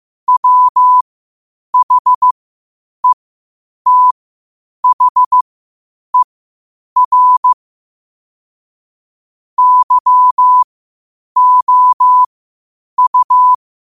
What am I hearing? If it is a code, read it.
Morse code, heard as a keyed tone, about 15 wpm, spaced as for 8 wpm, 1000 Hz. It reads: WHETHER YOU
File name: qotdfiles / 8wpm